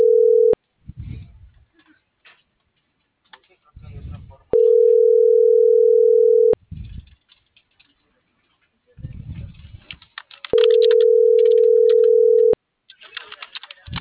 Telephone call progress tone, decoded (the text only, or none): ringback tone